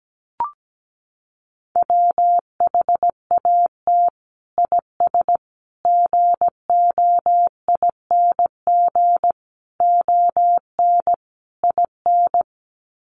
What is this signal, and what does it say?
Morse code, heard as a keyed tone, about 17 wpm, 700 Hz. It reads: WHAT IS GOING ON IN
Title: Sets of 5 Words Encore2 17wpm